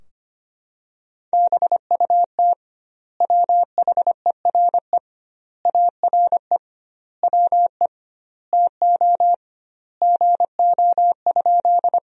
Morse code, heard as a keyed tone, about 25 wpm, 700 Hz. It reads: BUT WHERE ARE WE TO GO?
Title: 2x Word-Spacing - Sentences from Top 200 Words - 25wpm